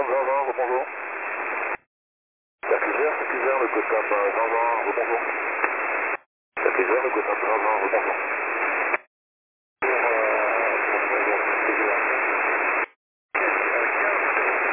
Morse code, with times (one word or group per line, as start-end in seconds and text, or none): none